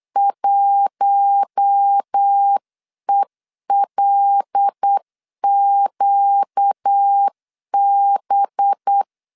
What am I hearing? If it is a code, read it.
Morse code, heard as a keyed tone, about 8 wpm, 800 Hz. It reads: 1ELQB
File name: CW / DET / 1 6 1500C S C 8wpm 5219